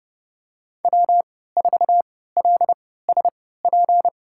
Morse code, heard as a keyed tone, about 30 wpm, spaced as for 17 wpm, 700 Hz. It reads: W4LSP
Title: USA Call Signs - 5 Characters - 3x Character Spacing - 30wpm